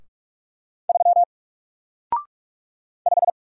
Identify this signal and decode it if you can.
Morse code, heard as a keyed tone, about 46 wpm, 700 Hz. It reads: 3 5
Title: Single Character (Letter:Number:Punctuation:Prosign) Alt 45wpm